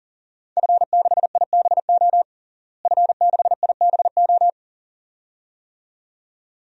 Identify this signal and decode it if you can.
Morse code, heard as a keyed tone, about 40 wpm, 700 Hz. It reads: F6IBO F6IBO
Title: DXCC-100 Call Signs - 5 Characters - Repeated 2x - 40wpm